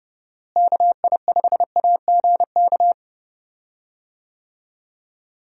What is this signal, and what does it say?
Morse code, heard as a keyed tone, about 32 wpm, 700 Hz. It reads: KI5AGK
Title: USA Call Signs - 6 Characters - Repeated 2x - 30wpm